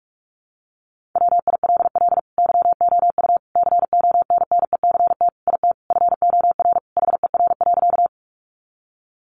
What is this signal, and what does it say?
Morse code, heard as a keyed tone, about 44 wpm, 700 Hz. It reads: WILL YOU CONNECT IT FOR HER.
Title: Sentences from Top 1500 Words 45wpm